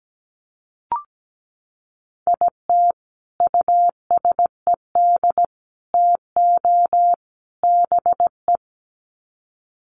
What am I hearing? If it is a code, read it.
Morse code, heard as a keyed tone, about 17 wpm, 700 Hz. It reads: IT USED TO BE
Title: Sets of 4 Words 17wpm